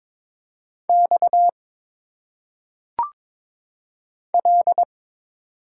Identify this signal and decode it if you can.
Morse code, heard as a keyed tone, about 22 wpm, 700 Hz. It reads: X L
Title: Single Letters - Phonetic Pronunciation 22wpm